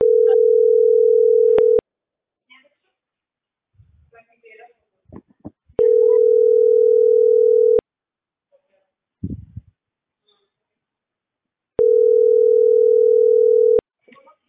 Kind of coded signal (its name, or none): telephone call progress tone